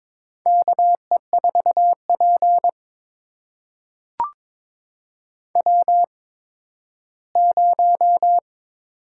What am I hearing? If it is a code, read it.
Morse code, heard as a keyed tone, about 22 wpm, spaced as for 13 wpm, 700 Hz. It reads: KE4P W0
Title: USA Call Signs - 4 Characters - 8x Character Spacing - 22wpm